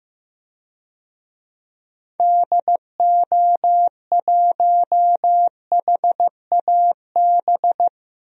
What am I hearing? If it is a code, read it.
Morse code, heard as a keyed tone, about 15 wpm, 700 Hz. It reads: DO1HAB
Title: DXCC-100 Call Signs - 6 Characters - 15wpm